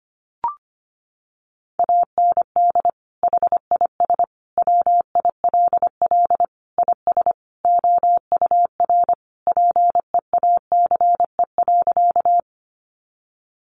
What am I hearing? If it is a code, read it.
Morse code, heard as a keyed tone, about 25 wpm, 700 Hz. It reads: AND HIS WILL IS OUR PEACE.
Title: Sentences from Top 1500 Words 25wpm